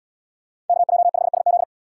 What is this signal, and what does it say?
Morse code, heard as a keyed tone, about 72 wpm, 700 Hz. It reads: KJ5SZ